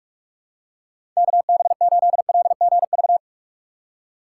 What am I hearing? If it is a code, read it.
Morse code, heard as a keyed tone, about 45 wpm, 700 Hz. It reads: KB8LGV
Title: USA Call Signs - 6 Characters - 2x Character Spacing - 45wpm